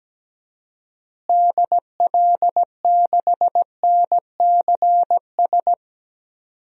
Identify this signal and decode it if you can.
Morse code, heard as a keyed tone, about 17 wpm, 700 Hz. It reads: DL6NCS
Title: DXCC-100 Call Signs - 6 Characters - Repeated 2x - 17wpm